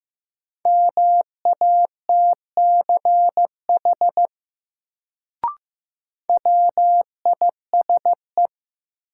Morse code, heard as a keyed tone, about 15 wpm, 700 Hz. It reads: MATCH WISE
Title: Sentences from Top 1600 Words - New Word Drill - 15wpm